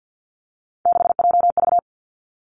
Morse code, heard as a keyed tone, about 50 wpm, 700 Hz. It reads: 6JV
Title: Three Challenging Letter-Number Combinations 50wpm